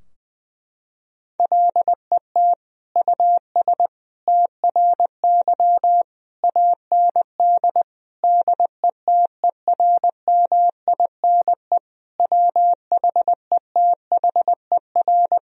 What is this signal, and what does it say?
Morse code, heard as a keyed tone, about 20 wpm, 700 Hz. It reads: LET US TRY AND DETERMINE WHETHER
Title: Sentences from Top 1000 Words - Review Part 4 - 20wpm